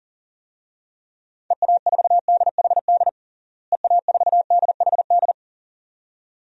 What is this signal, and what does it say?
Morse code, heard as a keyed tone, about 40 wpm, 700 Hz. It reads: EA4DHD EA4DHD
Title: DXCC-100 Call Signs - 6 Characters - Repeated 2x - 40wpm